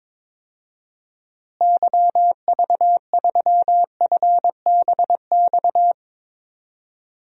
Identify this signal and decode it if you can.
Morse code, heard as a keyed tone, about 22 wpm, 700 Hz. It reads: YV3FBX
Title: DXCC-100 Call Signs - 6 Characters - Repeated 3x - 22wpm